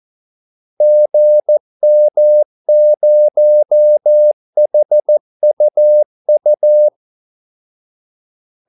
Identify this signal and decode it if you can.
Morse code, heard as a keyed tone, about 14 wpm, 600 Hz. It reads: GM0HUU
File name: Callsigns-14wpm